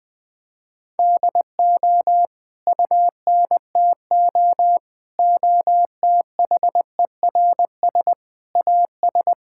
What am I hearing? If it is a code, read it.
Morse code, heard as a keyed tone, about 20 wpm, 700 Hz. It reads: DO UNTO OTHERS AS
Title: Sentences with 9-Word Phrases - Spoken Beginning - 20wpm